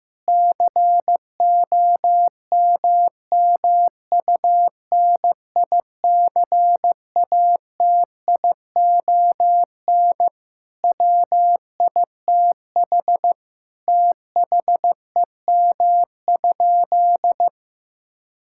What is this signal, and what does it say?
Morse code, heard as a keyed tone, about 16 wpm, 700 Hz. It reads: COMMUNICATION WITH THEM?
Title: Sentences from Top 1000 Words - Review Part 6 - 15wpm